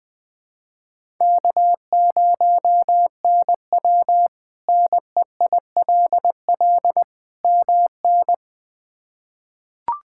Morse code, heard as a keyed tone, about 20 wpm, 700 Hz. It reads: K0NW NEILL MN
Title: K1USN SST QSO Practice - 20wpm